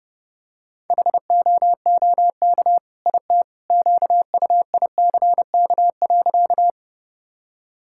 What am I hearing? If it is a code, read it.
Morse code, heard as a keyed tone, about 32 wpm, 700 Hz. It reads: HOOK IT QUICK.
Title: Sentences from Top 1600 Words 30wpm